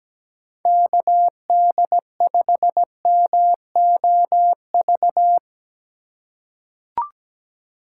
Morse code, heard as a keyed tone, about 17 wpm, 700 Hz. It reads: KD5MOV
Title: USA Call Signs - 4x Character Spacing - 17wpm